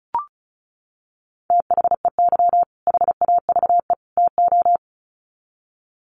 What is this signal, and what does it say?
Morse code, heard as a keyed tone, about 34 wpm, 700 Hz. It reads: THEY HAVE TO
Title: Sets of 3 Words 35wpm